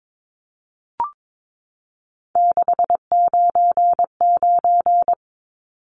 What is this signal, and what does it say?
Morse code, heard as a keyed tone, about 22 wpm, 700 Hz. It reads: 699
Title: Numbers - 3-Digit 22wpm